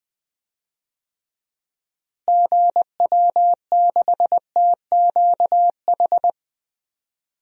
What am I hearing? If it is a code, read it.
Morse code, heard as a keyed tone, about 20 wpm, 700 Hz. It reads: GW6TQH
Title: DXCC-100 Call Signs - 6 Characters - Repeated 2x - 20wpm